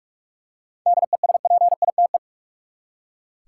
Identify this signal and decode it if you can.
Morse code, heard as a keyed tone, about 43 wpm, 700 Hz. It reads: DESPITE
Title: Seven Letter Words - 45wpm